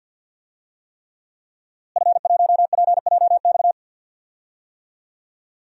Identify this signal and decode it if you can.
Morse code, heard as a keyed tone, about 52 wpm, 700 Hz. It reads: F1PJX
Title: DXCC-100 Call Signs - 5 Characters - Repeated 3x - 50wpm